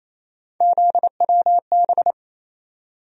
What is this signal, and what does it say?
Morse code, heard as a keyed tone, about 28 wpm, 700 Hz. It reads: ZWB
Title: USA Call Sign Postfixes - 3 Letters 28wpm